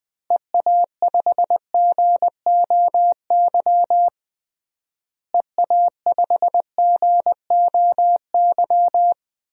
Morse code, heard as a keyed tone, about 20 wpm, 700 Hz. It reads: EA5GOY EA5GOY
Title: DXCC-100 Call Signs - 6 Characters - Repeated 3x - 20wpm